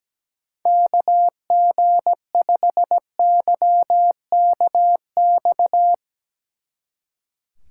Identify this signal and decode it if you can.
Morse code, heard as a keyed tone, about 18 wpm, 700 Hz. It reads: KG5YKX
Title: Speed Racing - USA 2x3 Call Signs 17wpm